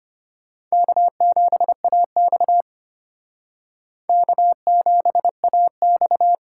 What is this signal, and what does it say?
Morse code, heard as a keyed tone, about 32 wpm, 700 Hz. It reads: K7AX K7AX